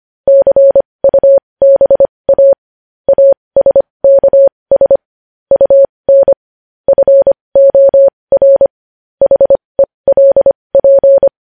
Morse code, heard as a keyed tone, about 25 wpm, spaced as for 20 wpm, 550 Hz. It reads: CUBA ASKS UN FOR HELP
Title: News Headlines at 20